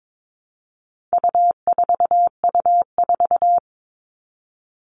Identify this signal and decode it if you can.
Morse code, heard as a keyed tone, about 22 wpm, 700 Hz. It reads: U4U4
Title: All Cut Numbers - 4-Digit 22wpm